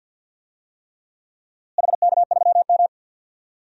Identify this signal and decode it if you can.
Morse code, heard as a keyed tone, about 51 wpm, 700 Hz. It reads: HK3M
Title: DXCC-100 Call Signs - 4 Characters - Repeated 3x - 50wpm